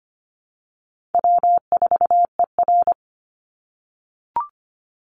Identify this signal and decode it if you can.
Morse code, heard as a keyed tone, about 25 wpm, 700 Hz. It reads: W4ER
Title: Easy Call Signs - 25wpm